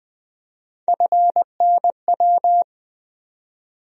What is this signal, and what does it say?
Morse code, heard as a keyed tone, about 20 wpm, 700 Hz. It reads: FNW
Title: USA Call Sign Postfixes - 3 Letters 20wpm